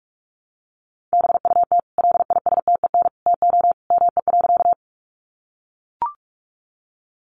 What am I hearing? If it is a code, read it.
Morse code, heard as a keyed tone, about 44 wpm, 700 Hz. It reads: BUT LISTEN TO ME.
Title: Sentences from Top 400 Words 45wpm